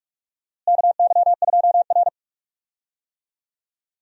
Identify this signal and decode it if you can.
Morse code, heard as a keyed tone, about 46 wpm, 700 Hz. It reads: KY2R
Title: USA Call Signs - 4 Characters - Repeated 3x - 45wpm